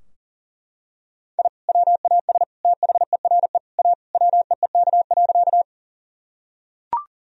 Morse code, heard as a keyed tone, about 40 wpm, 700 Hz. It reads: I WAS THERE A WEEK.
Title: Sentences from Top 1000 Words - Review Part 2 - 40wpm